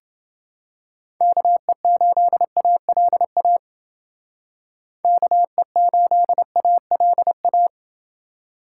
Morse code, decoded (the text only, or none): KE8ALA KE8ALA